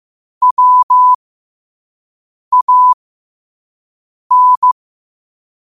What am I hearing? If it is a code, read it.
Morse code, heard as a keyed tone, about 15 wpm, spaced as for 5 wpm, 1000 Hz. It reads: WAN
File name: qotdfiles / 5wpm